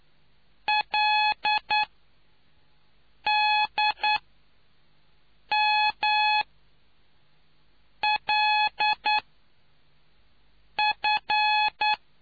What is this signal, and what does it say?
Morse code, heard as a keyed tone, about 9 wpm, spaced as for 4 wpm, 850 Hz. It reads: LDMLF